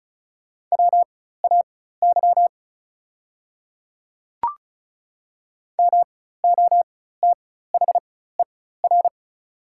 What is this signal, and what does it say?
Morse code, heard as a keyed tone, about 36 wpm, spaced as for 16 wpm, 700 Hz. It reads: WAY MOTHER
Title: 4x Character Spacing - Top 200 Words - 35wpm